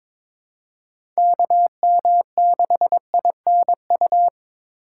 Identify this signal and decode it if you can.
Morse code, heard as a keyed tone, about 21 wpm, 700 Hz. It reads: KM6INU
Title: USA Call Signs - 6 Characters - 4x Character Spacing - 22wpm